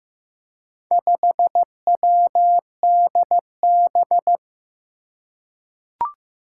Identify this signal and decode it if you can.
Morse code, heard as a keyed tone, about 15 wpm, 700 Hz. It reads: 5WDB
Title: Four Challenging Letter-Number Combinations 15wpm